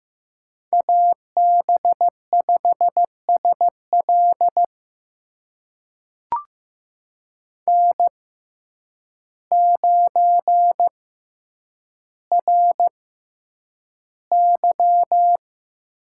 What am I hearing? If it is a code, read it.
Morse code, heard as a keyed tone, about 15 wpm, spaced as for 8 wpm, 700 Hz. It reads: AB5SL N9RY
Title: USA Call Signs - 5 Characters - 6x Character Spacing - 15wpm